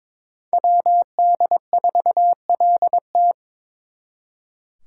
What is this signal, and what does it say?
Morse code, heard as a keyed tone, about 26 wpm, 700 Hz. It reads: WD4LT
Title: Speed Racing - USA Call Signs 22wpm